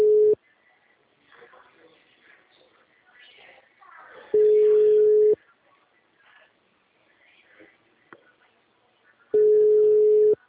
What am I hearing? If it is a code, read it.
Ringback tone (Europe, 425 Hz), with speech.